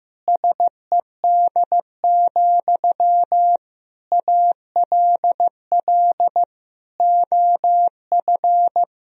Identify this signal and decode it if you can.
Morse code, heard as a keyed tone, about 15 wpm, 700 Hz. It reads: SED, ALL OF
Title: Sentences from Top 500 Words and Abbreviations 15wpm